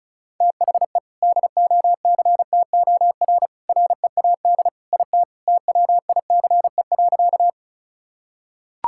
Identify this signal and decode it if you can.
Morse code, heard as a keyed tone, about 36 wpm, 700 Hz. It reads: THE DOCTOR READ IT TWICE.
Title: Sentences from Top 1600 Words 35wpm